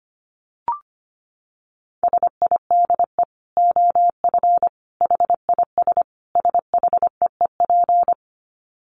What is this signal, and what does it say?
Morse code, heard as a keyed tone, about 25 wpm, 700 Hz. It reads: SIDE OF HIS SHEEP
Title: Dit Challenge 25wpm